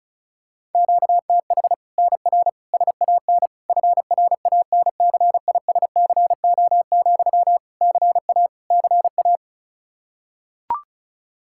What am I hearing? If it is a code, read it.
Morse code, heard as a keyed tone, about 36 wpm, 700 Hz. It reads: QTH NR SAN FRANCISCO, CA CA
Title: Practice QSO - Exchange 1 - Beginning - 35wpm